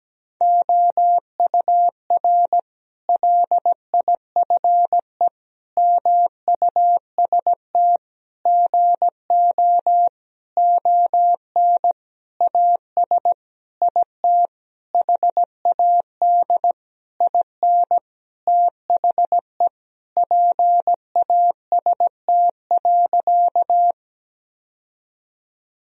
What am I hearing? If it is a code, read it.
Morse code, heard as a keyed tone, about 17 wpm, 700 Hz. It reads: OUR LIFE MUST GO ON AS IT HAD IN THE PAST.
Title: Sentences from Top 1000 Words - Review Part 3 - 17wpm